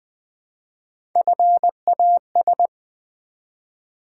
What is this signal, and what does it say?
Morse code, heard as a keyed tone, about 20 wpm, 700 Hz. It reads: FAS
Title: Build A Word - 1 Letter At A Time - 4-Letter Words - 20wpm